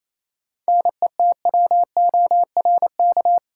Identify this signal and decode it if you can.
Morse code, heard as a keyed tone, about 28 wpm, 700 Hz. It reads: NETWORK
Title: Sentences from Top 1000 Words - Review Part 7 - New Word Drill - 28wpm